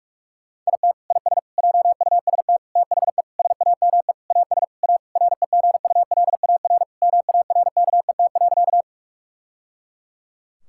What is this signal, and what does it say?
Morse code, heard as a keyed tone, about 46 wpm, 700 Hz. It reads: IT IS JUST THE SAME AS A REGULAR MARKET.
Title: Sentences from Top 1000 Words - Review Part 6 - 45wpm